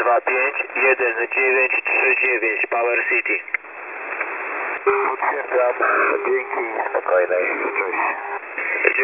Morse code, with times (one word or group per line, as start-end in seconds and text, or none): none